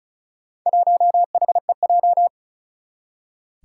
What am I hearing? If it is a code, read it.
Morse code, heard as a keyed tone, about 36 wpm, 700 Hz. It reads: 1HEJ
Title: Four Challenging Letter-Number Combinations 35wpm